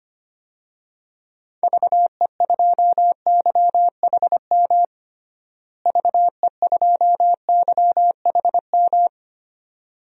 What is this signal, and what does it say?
Morse code, heard as a keyed tone, about 25 wpm, 700 Hz. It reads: VE2YHM VE2YHM
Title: DXCC-100 Call Signs - 6 Characters - Repeated 2x - 25wpm